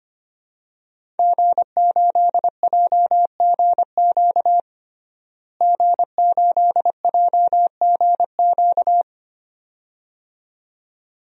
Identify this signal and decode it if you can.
Morse code, heard as a keyed tone, about 25 wpm, 700 Hz. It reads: G8JGQ G8JGQ
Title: DXCC-100 Call Signs - 5 Characters - Repeated 2x - 25wpm